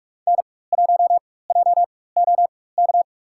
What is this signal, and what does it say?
Morse code, heard as a keyed tone, about 43 wpm, spaced as for 21 wpm, 700 Hz. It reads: N1JOK
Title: USA Call Signs - 5 Characters - 4x Character Spacing - 45wpm